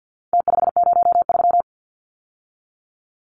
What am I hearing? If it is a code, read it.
Morse code, heard as a keyed tone, about 49 wpm, 700 Hz. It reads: T503
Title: Numbers - Contest Style - 4-Digit 50wpm